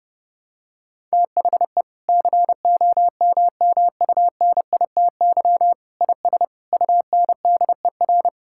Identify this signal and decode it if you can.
Morse code, heard as a keyed tone, about 30 wpm, 700 Hz. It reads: THE COMMUNITY IS UNDER
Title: Sentences from Top 1000 Words - Review Part 6 - 30wpm